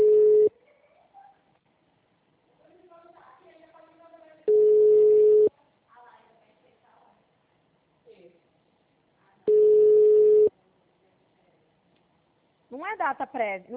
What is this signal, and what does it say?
Ringback tone (Europe, 425 Hz), with speech.